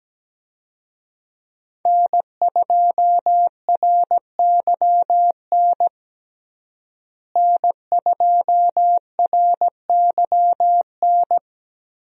Morse code, heard as a keyed tone, about 17 wpm, 700 Hz. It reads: N2RYN N2RYN
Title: USA Call Signs - 5 Characters - Repeated 3x - 17wpm